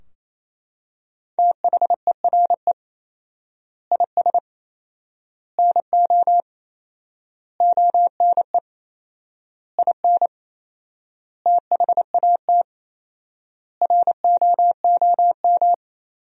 Morse code, heard as a keyed tone, about 28 wpm, 700 Hz. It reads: THERE IS NO ONE IN THAT ROOM
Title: 4x Word-Spacing - Sentences from Top 500 Words - 28wpm